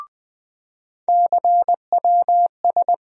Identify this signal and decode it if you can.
Morse code, heard as a keyed tone, about 20 wpm, 700 Hz. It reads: CWS
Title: USA Call Sign Postfixes - 3 Letters 20wpm